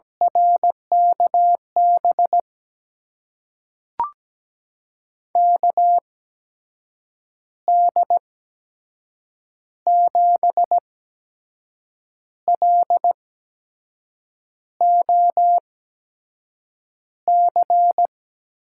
Morse code, heard as a keyed tone, about 17 wpm, spaced as for 6 wpm, 700 Hz. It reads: RKB KD7LOC
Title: USA Call Signs - 6 Characters - 8x Character Spacing - 17wpm